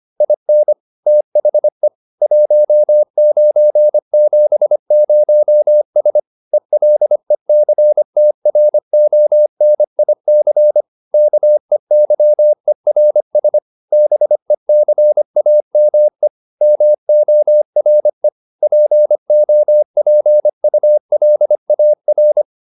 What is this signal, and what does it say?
Morse code, heard as a keyed tone, about 25 wpm, 600 Hz. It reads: IN THE 1970S ELECTRONIC KEYERS BECAME MORE POPULAR